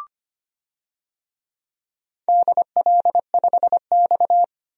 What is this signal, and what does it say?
Morse code, heard as a keyed tone, about 25 wpm, 700 Hz. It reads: DL5X
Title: DXCC-100 Call Signs - 4 Characters - 25wpm